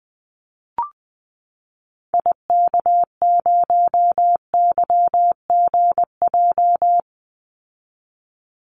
Morse code, heard as a keyed tone, about 20 wpm, 700 Hz. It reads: IK0YGJ
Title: Call Signs Repeated 2x - 20wpm